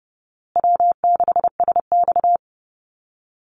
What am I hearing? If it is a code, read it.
Morse code, heard as a keyed tone, about 30 wpm, 700 Hz. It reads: W6SX
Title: Easy Call Signs Repeated 3x - 30wpm